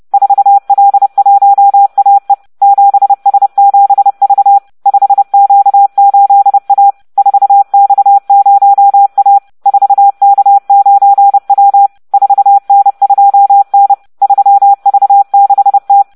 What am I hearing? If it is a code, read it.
Morse code, heard as a keyed tone, about 30 wpm, 800 Hz. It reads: 4L1AE 7S7V 5Q8A 4X0A 4K9W 4N2N 3V6T